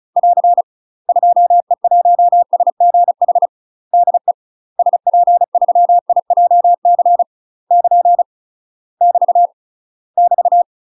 Morse code, heard as a keyed tone, about 34 wpm, 700 Hz. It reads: <AR> 2E1SGH DE SP3IJC <KN> <BT> <BT>
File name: UK MASTER QSOs TRAINING / 35WPM / UK MASTER QSOs 0031